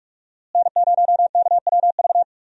Morse code, heard as a keyed tone, about 46 wpm, 700 Hz. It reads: N0KWV
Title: USA Call Signs - 5 Characters - Repeated 3x - 45wpm